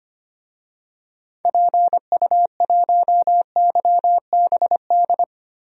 Morse code, heard as a keyed tone, about 25 wpm, 700 Hz. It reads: PU1YBD
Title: DXCC-100 Call Signs - 6 Characters - 25wpm